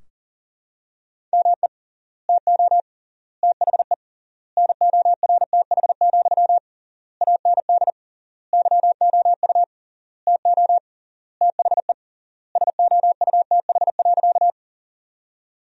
Morse code, heard as a keyed tone, about 40 wpm, 700 Hz. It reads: ME TO THE NORTH, AND YOU TO THE SOUTH.
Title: 3x Word-Spacing - Sentences from Top 400 Words - 40wpm